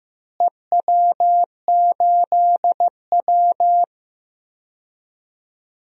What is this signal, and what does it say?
Morse code, heard as a keyed tone, about 15 wpm, 700 Hz. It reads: EW8W
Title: DXCC-100 Call Signs - 4 Characters - Repeated 3x - 15wpm